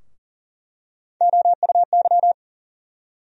Morse code, heard as a keyed tone, about 40 wpm, 700 Hz. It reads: OUY
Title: USA Call Sign Postfixes - 3 Letters 40wpm